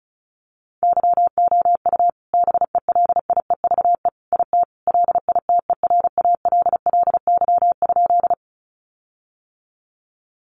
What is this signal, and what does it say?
Morse code, heard as a keyed tone, about 34 wpm, 700 Hz. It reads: YOU BELIEVE IT LITERALLY?
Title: Sentences from Top 1400 Words 35wpm